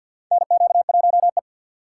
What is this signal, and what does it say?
Morse code, heard as a keyed tone, about 49 wpm, 700 Hz. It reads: NQ1E
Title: USA Call Signs - 6x Character Spacing - 50wpm